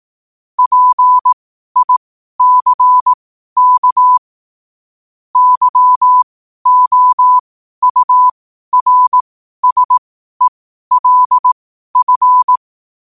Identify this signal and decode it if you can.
Morse code, heard as a keyed tone, about 18 wpm, spaced as for 12 wpm, 1000 Hz. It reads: PICK YOURSELF
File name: qotdfiles / 12wpm